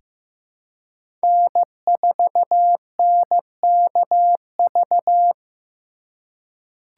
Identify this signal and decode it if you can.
Morse code, heard as a keyed tone, about 15 wpm, 700 Hz. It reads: N4NKV
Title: USA Call Signs - 4x Character Spacing - 15wpm